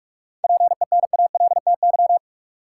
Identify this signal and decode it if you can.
Morse code, heard as a keyed tone, about 46 wpm, 700 Hz. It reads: PENALTY
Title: Seven Letter Words - 45wpm